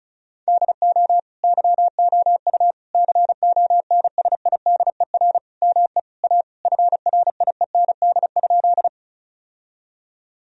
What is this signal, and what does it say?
Morse code, heard as a keyed tone, about 36 wpm, 700 Hz. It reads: DO YOU CONSIDER ME A FRIEND?
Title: Sentences from Top 1000 Words - Review Part 4 - 35wpm